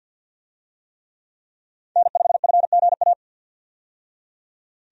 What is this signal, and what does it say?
Morse code, heard as a keyed tone, about 47 wpm, 700 Hz. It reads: N5FGA
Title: USA Call Signs - 5 Characters - 50wpm